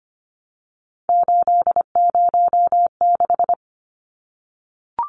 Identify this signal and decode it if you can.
Morse code, heard as a keyed tone, about 25 wpm, 700 Hz. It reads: 806
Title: Numbers - 3-Digit 25wpm